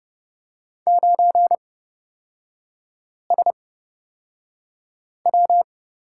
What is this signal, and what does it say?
Morse code, heard as a keyed tone, about 30 wpm, spaced as for 5 wpm, 700 Hz. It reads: 9SW
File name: Lesson 38e - Letter q - Three Characters 30wpm